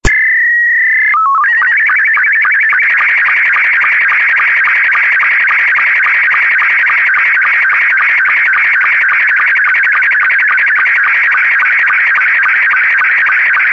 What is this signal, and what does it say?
SSTV (slow-scan television), Scottie 2.
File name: sstv / sstv2 / scottie s2.48k